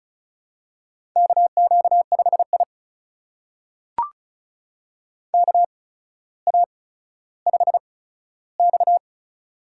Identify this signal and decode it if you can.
Morse code, heard as a keyed tone, about 36 wpm, spaced as for 15 wpm, 700 Hz. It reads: KQ5I KA5X
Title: USA Call Signs - 4 Characters - 8x Character Spacing - 35wpm